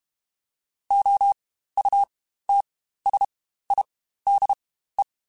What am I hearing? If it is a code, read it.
Morse code, heard as a keyed tone, about 32 wpm, spaced as for 15 wpm, 800 Hz. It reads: OUTSIDE
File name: Spaced Characters / Spaced Characters-mp3s / Weather 4 14 sp @30